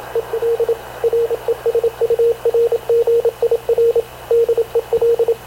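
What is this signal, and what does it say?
Morse code, heard as a keyed tone, about 26 wpm, 460 Hz, with radio noise. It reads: EL RESURGIR DEL
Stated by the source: Havana Radio CLA press (PX) broadcast, recorded by Radio Officer Paul Dumesnil aboard mv CNAV QUEST/CZDO.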